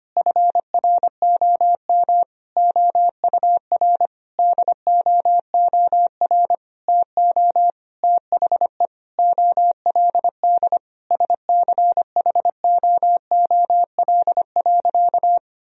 Morse code, heard as a keyed tone, about 25 wpm, 700 Hz. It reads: FROM OUR DOOR TO THE OLD SCHOOL.
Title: Sentences from Top 1000 Words - Review Part 1 - 25wpm